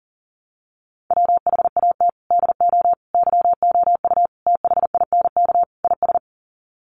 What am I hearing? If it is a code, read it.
Morse code, heard as a keyed tone, about 40 wpm, 700 Hz. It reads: WHAT DO YOU THINK IS
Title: Sets of 5 Words Encore 40wpm